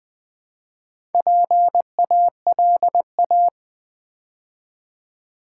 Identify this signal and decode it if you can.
Morse code, heard as a keyed tone, about 20 wpm, 700 Hz. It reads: PALA
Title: Build A Word - 2 Letters At A Time - 6-Letter Words - 20wpm